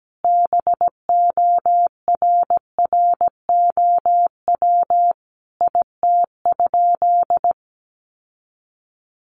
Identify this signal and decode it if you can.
Morse code, heard as a keyed tone, about 17 wpm, 700 Hz. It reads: BORROW IT?
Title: Sentences from Top 1500 Words 17wpm